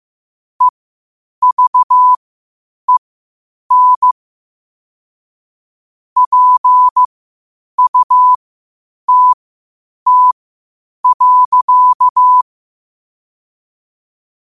Morse code, heard as a keyed tone, about 15 wpm, spaced as for 8 wpm, 1000 Hz. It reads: EVEN PUTT.
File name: qotdfiles / 8wpm